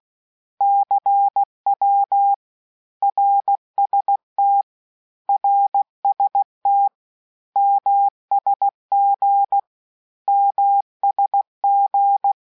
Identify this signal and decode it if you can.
Morse code, heard as a keyed tone, about 16 wpm, 800 Hz. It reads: CW RST RST MSG MSG
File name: cours-du-lundi files / cours-1580-abreviations-mpm-15